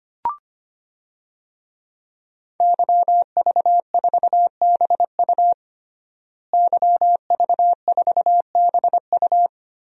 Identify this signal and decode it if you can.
Morse code, heard as a keyed tone, about 25 wpm, 700 Hz. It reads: YV4BU YV4BU
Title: DXCC-100 Call Signs - 5 Characters - Repeated 3x - 25wpm